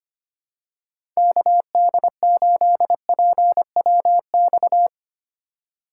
Morse code, heard as a keyed tone, about 25 wpm, 700 Hz. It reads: KD8PWX
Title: USA Call Signs - 6 Characters - 3x Character Spacing - 25wpm